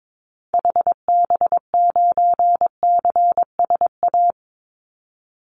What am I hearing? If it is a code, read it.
Morse code, heard as a keyed tone, about 22 wpm, 700 Hz. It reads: HB9CSA
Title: Call Signs - 22wpm